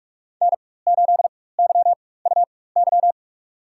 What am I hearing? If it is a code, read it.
Morse code, heard as a keyed tone, about 42 wpm, spaced as for 21 wpm, 700 Hz. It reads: N8YUY
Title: USA Call Signs - 4x Character Spacing - 45wpm